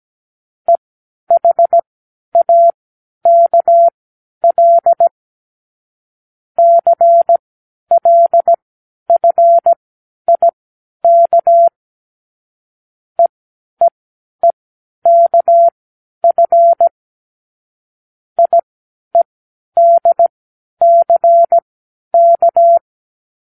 Morse code, heard as a keyed tone, about 17 wpm, spaced as for 10 wpm, 680 Hz. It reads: EHAKL CLFIK EEEKF IEDCK
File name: k7qo / Lesson18(A-L)